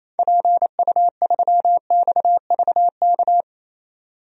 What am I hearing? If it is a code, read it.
Morse code, heard as a keyed tone, about 28 wpm, 700 Hz. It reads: PU3XVK
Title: DXCC-100 Call Signs - 6 Characters - 28wpm